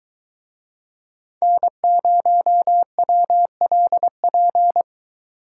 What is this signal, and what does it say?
Morse code, heard as a keyed tone, about 23 wpm, 700 Hz. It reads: N0WLP